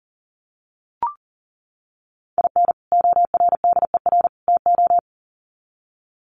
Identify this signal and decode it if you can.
Morse code, heard as a keyed tone, about 40 wpm, 700 Hz. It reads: IN ORDER TO
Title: Sets of 3 Words 40wpm